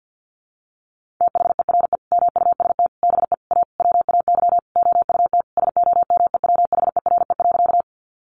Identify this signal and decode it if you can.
Morse code, heard as a keyed tone, about 50 wpm, 700 Hz. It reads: THERE MUST BE A WAY OUT SOMEWHERE.
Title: Sentences from Top 1400 Words 50wpm